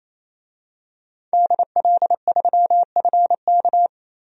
Morse code, heard as a keyed tone, about 28 wpm, 700 Hz. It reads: DL3FK
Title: DXCC-100 Call Signs - 5 Characters - 28wpm